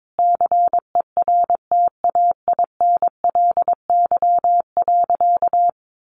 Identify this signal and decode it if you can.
Morse code, heard as a keyed tone, about 22 wpm, 700 Hz. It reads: CERTAINLY.
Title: Sentences from Top 900 Words 22wpm